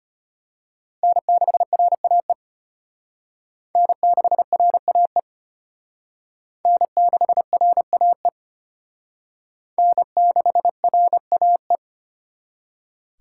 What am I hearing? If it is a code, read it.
Morse code, heard as a keyed tone, about 31 wpm, 700 Hz. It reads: N6RAE N6RAE N6RAE N6RAE
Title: Speed Racing - USA 1x3 Call Signs 25wpm